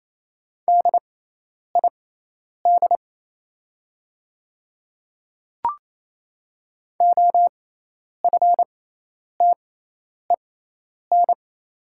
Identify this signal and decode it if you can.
Morse code, heard as a keyed tone, about 28 wpm, spaced as for 10 wpm, 690 Hz. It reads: DID OFTEN
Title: 6x Character Spacing - Top 300 Words - 28wpm